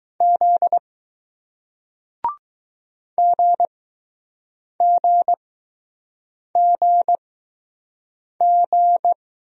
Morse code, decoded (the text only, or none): Z GGGG